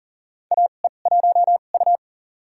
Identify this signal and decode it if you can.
Morse code, heard as a keyed tone, about 38 wpm, spaced as for 28 wpm, 690 Hz. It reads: AE1U
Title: USA Call Signs - 4 Characters - 2x Character Spacing - 40wpm